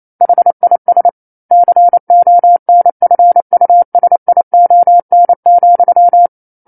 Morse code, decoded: HIS CONFUSION,